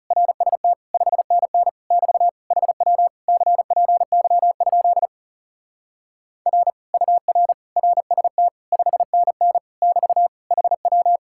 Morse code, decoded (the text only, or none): RST 5NN <BT> HW CPY? R UR RST 5NN <BT> HW